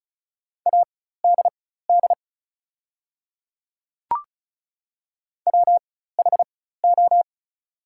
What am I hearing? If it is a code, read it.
Morse code, heard as a keyed tone, about 34 wpm, spaced as for 16 wpm, 700 Hz. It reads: ADD WHO
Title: 4x Character Spacing - Top 300 Words - 35wpm